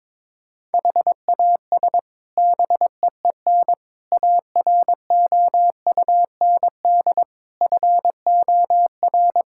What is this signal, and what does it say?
Morse code, heard as a keyed tone, about 22 wpm, 700 Hz. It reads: HAS BEEN AROUND FOR
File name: practice / 22wpm / Sets of 7 Words - 22wpm